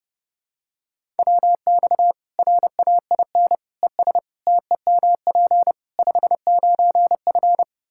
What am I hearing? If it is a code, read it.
Morse code, heard as a keyed tone, about 30 wpm, 700 Hz. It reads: WX RAIN ES TEMP 59F
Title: Practice QSO - Exchange 2 - Info - 30wpm